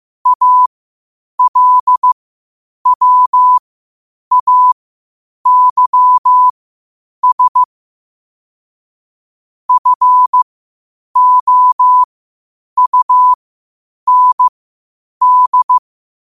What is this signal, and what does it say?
Morse code, heard as a keyed tone, about 15 wpm, spaced as for 8 wpm, 1000 Hz. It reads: ALWAYS FOUND
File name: qotdfiles / 8wpm